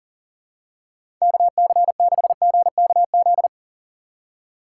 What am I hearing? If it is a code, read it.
Morse code, heard as a keyed tone, about 40 wpm, 700 Hz. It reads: KC6GKZ
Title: USA Call Signs - 2x Character Spacing - 40wpm